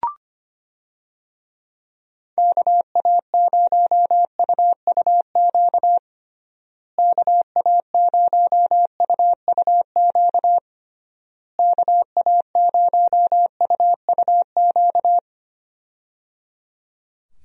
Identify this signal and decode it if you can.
Morse code, heard as a keyed tone, about 25 wpm, 700 Hz. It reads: KA0UUQ KA0UUQ KA0UUQ
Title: USA Call Signs - 6 Characters - Repeated 3x - 25wpm